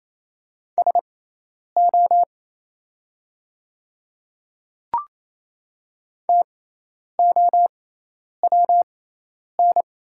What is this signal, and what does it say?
Morse code, heard as a keyed tone, about 28 wpm, spaced as for 10 wpm, 700 Hz. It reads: SO TOWN
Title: 6x Character Spacing - Top 500 Words - 28wpm